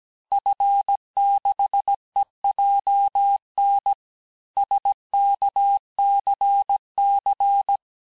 Morse code, heard as a keyed tone, about 17 wpm, 800 Hz. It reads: F6EJN SKCC
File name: cours-du-lundi files / cours-1587-FIN-de-QSO-vitesse-16MPM